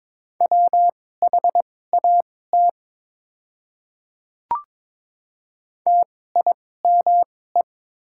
Morse code, heard as a keyed tone, about 23 wpm, spaced as for 16 wpm, 700 Hz. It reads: WHAT TIME
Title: 2x Character Spacing - Top 200 Words - 22wpm